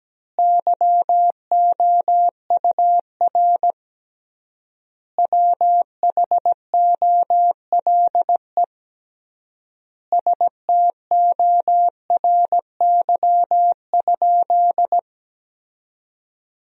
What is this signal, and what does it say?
Morse code, heard as a keyed tone, about 17 wpm, 700 Hz. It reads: YOUR WHOLE STORY?
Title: 3x Word-Spacing - Sentences from Top 400 Words - 17wpm